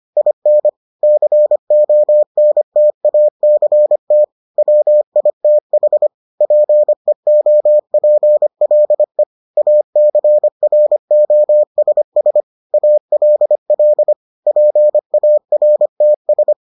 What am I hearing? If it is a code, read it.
Morse code, heard as a keyed tone, about 25 wpm, 600 Hz. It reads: IN CONTACT WITH PEOPLE ACROSS ALL PARTS